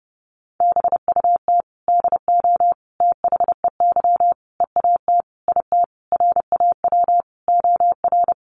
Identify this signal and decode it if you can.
Morse code, heard as a keyed tone, about 30 wpm, 700 Hz. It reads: BUT DO THEY EAT IT RAW OR
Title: Sentences from Top 1500 Words 30wpm